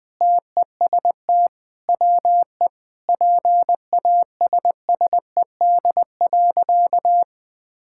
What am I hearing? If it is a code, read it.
Morse code, heard as a keyed tone, about 20 wpm, 700 Hz. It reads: TEST WE PASSED.
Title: Sentences with 4-Word Phrases - Spoken Beginning - 20wpm